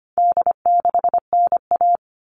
Morse code, heard as a keyed tone, about 25 wpm, 700 Hz. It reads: D6NA
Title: All Cut Numbers - 4-Digit 25wpm